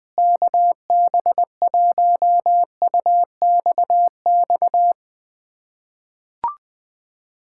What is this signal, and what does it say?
Morse code, heard as a keyed tone, about 21 wpm, 700 Hz. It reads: KB1UXX
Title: USA Callsign DE Callsign - 6 Characters - 20wpm